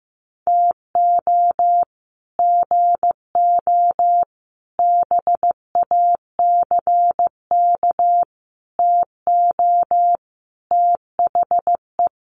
Morse code, heard as a keyed tone, about 15 wpm, 700 Hz. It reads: TO GO BACK TO THE
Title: Sets of 5 Words 15wpm